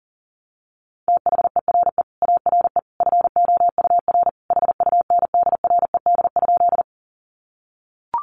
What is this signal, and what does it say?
Morse code, heard as a keyed tone, about 40 wpm, 700 Hz. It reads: THERE ARE FOUR HUNDRED?
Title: Sentences from Top 400 Words 40wpm